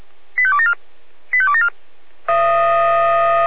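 Five-tone selective call (ZVEI), heard as quick runs of five tones, tones of about 70 ms. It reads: 85274 85274
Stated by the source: ZVEI five-tone selective call received on an SDR.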